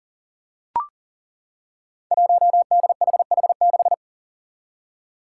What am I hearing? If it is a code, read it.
Morse code, heard as a keyed tone, about 40 wpm, 700 Hz. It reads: 1DHH6
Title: Five Challenging Letter-Number Combinations 40wpm